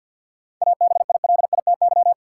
Morse code, heard as a keyed tone, about 53 wpm, 700 Hz. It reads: ABILITY
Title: Sets of 3 Words - New Word Drill - 50wpm